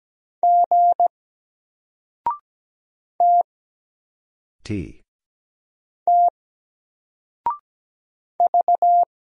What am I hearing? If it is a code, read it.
Morse code, heard as a keyed tone, about 17 wpm, 700 Hz. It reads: G T T V
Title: Single Character (Letter:Number:Punctuation:Prosign) 17wpm